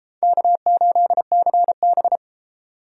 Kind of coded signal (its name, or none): Morse code